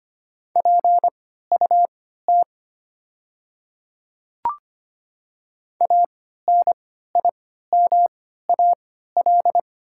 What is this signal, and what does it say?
Morse code, heard as a keyed tone, about 25 wpm, spaced as for 14 wpm, 700 Hz. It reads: PUT ANIMAL
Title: 3x Character Spacing - Top 200 Words - 25wpm